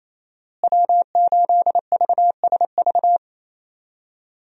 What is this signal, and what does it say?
Morse code, heard as a keyed tone, about 28 wpm, 700 Hz. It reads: W8VSV
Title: USA Call Signs - 5 Characters - Repeated 2x - 28wpm